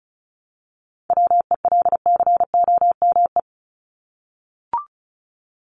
Morse code, heard as a keyed tone, about 34 wpm, 700 Hz. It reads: WELCOME
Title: Top 1751-2000 Words 35wpm